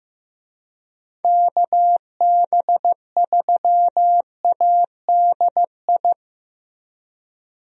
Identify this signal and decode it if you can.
Morse code, heard as a keyed tone, about 15 wpm, 700 Hz. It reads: KB3ADI
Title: USA Call Signs - 6 Characters - Repeated 2x - 15wpm